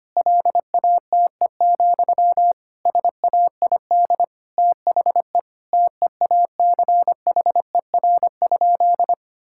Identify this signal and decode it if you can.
Morse code, heard as a keyed tone, about 25 wpm, 700 Hz. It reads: LATE, SAID THE TEACHER?
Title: Sentences from Top 1000 Words - Review Part 6 - 25wpm